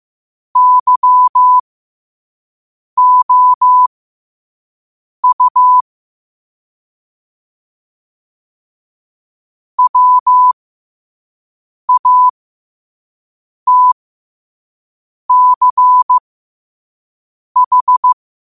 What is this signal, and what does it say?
Morse code, heard as a keyed tone, about 15 wpm, spaced as for 5 wpm, 1000 Hz. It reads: YOU WATCH
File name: qotdfiles / 5wpm